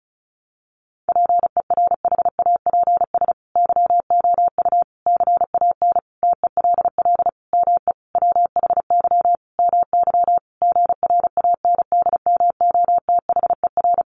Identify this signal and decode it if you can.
Morse code, heard as a keyed tone, about 34 wpm, 700 Hz. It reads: PERHAPS YOU CAN TELL ME WHY MY GRANDMOTHER
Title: Sentences from Top 1200 Words 35wpm